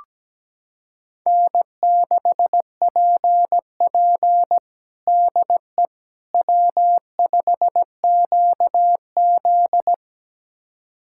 Morse code, heard as a keyed tone, about 17 wpm, 700 Hz. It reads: N6PP DE W5QZ
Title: USA Callsign DE Callsign - 4 Characters - 17wpm